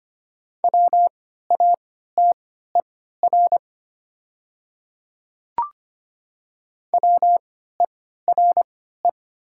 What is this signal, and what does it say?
Morse code, heard as a keyed tone, about 25 wpm, spaced as for 14 wpm, 700 Hz. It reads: WATER WERE
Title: 3x Character Spacing - Top 200 Words - 25wpm